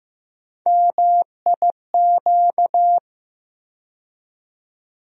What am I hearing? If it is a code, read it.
Morse code, heard as a keyed tone, about 15 wpm, 700 Hz. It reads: MIQ
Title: USA Call Sign Postfixes - 3 Letters 15wpm